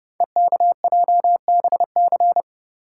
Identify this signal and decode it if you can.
Morse code, heard as a keyed tone, about 28 wpm, 700 Hz. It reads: EKJBC
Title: Code Groups - 5 Letters - 30wpm